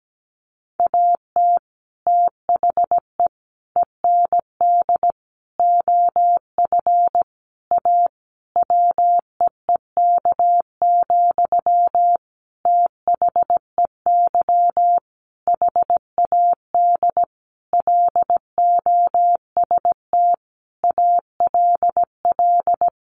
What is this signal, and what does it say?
Morse code, heard as a keyed tone, about 17 wpm, 700 Hz. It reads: AT THE END OF A WEEK, THEY HAD LOST ALL
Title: Sentences from Top 1400 Words 17wpm